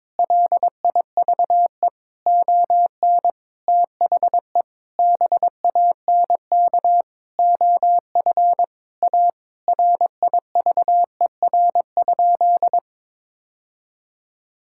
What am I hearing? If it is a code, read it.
Morse code, heard as a keyed tone, about 22 wpm, 700 Hz. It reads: LIVE ON THE BANK OF A RIVER?
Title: Sentences from Top 1000 Words - Review Part 6 - 22wpm